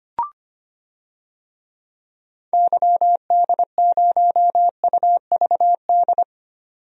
Morse code, heard as a keyed tone, about 25 wpm, 700 Hz. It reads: YD0UVD
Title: DXCC-100 Call Signs - 6 Characters - 25wpm